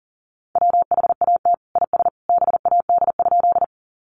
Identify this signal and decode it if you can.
Morse code, heard as a keyed tone, about 37 wpm, 700 Hz. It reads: WHAT IS BAD?
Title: Sentences from Top 700 Words 40wpm